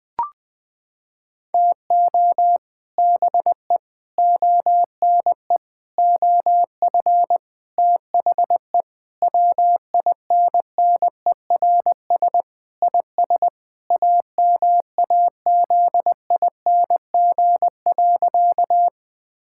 Morse code, heard as a keyed tone, about 20 wpm, 700 Hz. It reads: TO BE ONE OF THE WINNERS IS AMAZING.
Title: Sentences with 5-Word Phrases - 20wpm